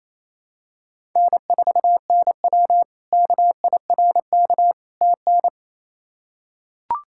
Morse code, heard as a keyed tone, about 28 wpm, 700 Hz. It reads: N4NW KIRK TN
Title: K1USN SST QSO Practice - 28wpm